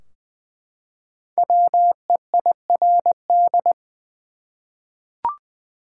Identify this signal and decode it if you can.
Morse code, heard as a keyed tone, about 20 wpm, 700 Hz. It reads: WEIRD
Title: Five Letter Words - 20wpm